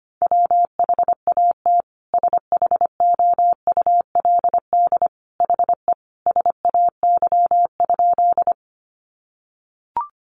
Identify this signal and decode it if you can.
Morse code, heard as a keyed tone, about 25 wpm, 700 Hz. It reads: WHAT SHOULD HE SAY?
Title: Sentences from Top 300 Words 25wpm